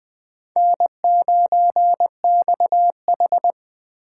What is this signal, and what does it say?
Morse code, heard as a keyed tone, about 24 wpm, 700 Hz. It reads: N9XH
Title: Speed Racing - USA 1x2 Call Signs 20wpm